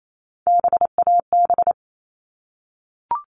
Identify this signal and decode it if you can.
Morse code, heard as a keyed tone, about 28 wpm, 700 Hz. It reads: BAB
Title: All Cut Numbers - 3-Digit 28wpm